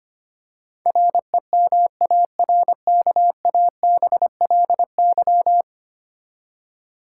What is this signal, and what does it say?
Morse code, heard as a keyed tone, about 25 wpm, 700 Hz. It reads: REMARKABLY
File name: Ten Letter Words - 25wpm